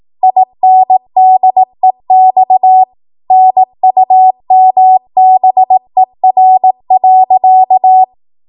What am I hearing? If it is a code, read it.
Morse code, heard as a keyed tone, about 18 wpm, 750 Hz. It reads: INDEX NUMBER.
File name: Archive / 18 WPM / 18WPM